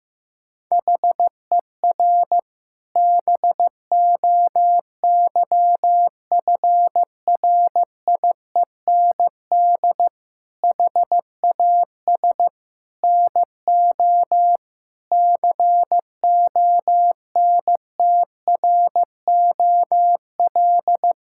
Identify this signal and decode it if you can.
Morse code, heard as a keyed tone, about 15 wpm, 700 Hz. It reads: HER BOYFRIEND HAS NO CONTROL
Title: Sentences from Top 1000 Words - Review Part 5 - 15wpm